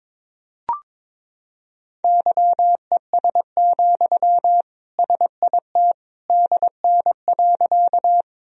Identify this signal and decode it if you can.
Morse code, heard as a keyed tone, about 22 wpm, 700 Hz. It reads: YES, SIT DN.
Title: Sentences from Top 500 Words and Abbreviations 22wpm